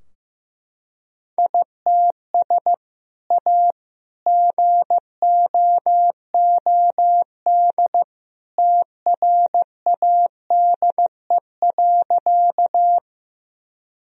Morse code, heard as a keyed tone, about 15 wpm, 700 Hz. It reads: ITS A GOOD TRADE.
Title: Sentences from Top 1000 Words - Review Part 4 - 15wpm